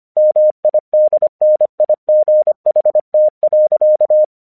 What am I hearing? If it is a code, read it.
Morse code, heard as a keyed tone, about 25 wpm, 610 Hz. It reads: MIDNIGHT.
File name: 1 English 25wpm / AQC28